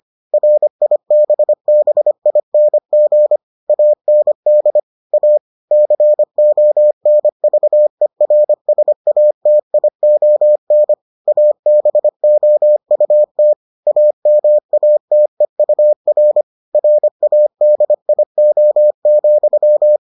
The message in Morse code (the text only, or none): RIBBING AND A CONVERSATION ABOUT AMATEUR RADIO,